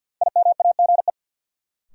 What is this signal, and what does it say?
Morse code, heard as a keyed tone, about 50 wpm, 700 Hz. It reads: IMAGE
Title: Sentences from Top 700 Words - New Word Drill - 50wpm